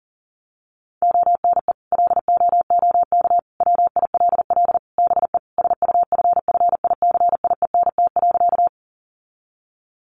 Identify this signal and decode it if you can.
Morse code, heard as a keyed tone, about 40 wpm, 700 Hz. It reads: ONE LOOK WILL BE SUFFICIENT.
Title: Sentences from Top 1400 Words 40wpm